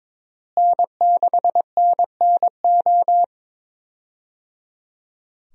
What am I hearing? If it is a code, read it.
Morse code, heard as a keyed tone, about 22 wpm, 700 Hz. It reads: N6NNO
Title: USA Call Signs - 5 Characters - Repeated 3x - 22wpm